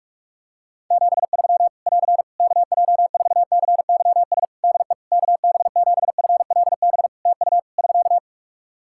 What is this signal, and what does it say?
Morse code, heard as a keyed tone, about 46 wpm, 700 Hz. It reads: 73 <AR> KJ4CYS DE KB7FLB TU <SK>
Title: Practice QSO - Exchange 4 - Ending - 45wpm